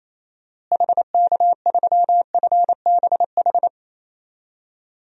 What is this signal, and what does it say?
Morse code, heard as a keyed tone, about 27 wpm, 700 Hz. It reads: HK3FBH
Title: DXCC-100 Call Signs - 6 Characters - Repeated 2x - 28wpm